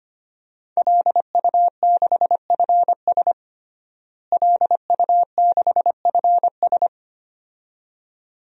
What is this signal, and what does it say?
Morse code, heard as a keyed tone, about 25 wpm, 700 Hz. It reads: LU6FS LU6FS
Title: DXCC-100 Call Signs - 5 Characters - Repeated 2x - 25wpm